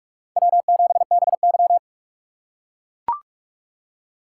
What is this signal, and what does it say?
Morse code, heard as a keyed tone, about 43 wpm, 700 Hz. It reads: W7BY